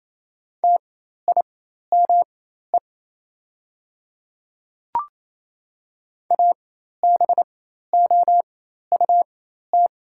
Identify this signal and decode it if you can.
Morse code, heard as a keyed tone, about 28 wpm, spaced as for 13 wpm, 700 Hz. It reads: TIME ABOUT
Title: 4x Character Spacing - Top 100 Words - 28wpm